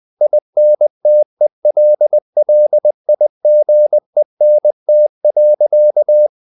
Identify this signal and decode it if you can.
Morse code, heard as a keyed tone, about 20 wpm, 600 Hz. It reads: INTELLIGENT.
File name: the old man and the sea part 2 20wpm